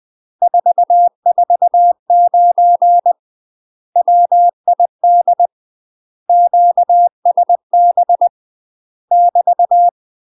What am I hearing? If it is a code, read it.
Morse code, heard as a keyed tone, about 20 wpm, 700 Hz. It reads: 449 WID QSB <BT>
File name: UK MASTER QSOs TRAINING / 20WPM / UK MASTER QSOs 0048